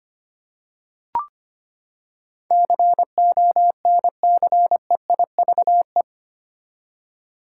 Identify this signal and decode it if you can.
Morse code, heard as a keyed tone, about 25 wpm, 700 Hz. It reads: CONCEIVE
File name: Eight Letter Words - 25wpm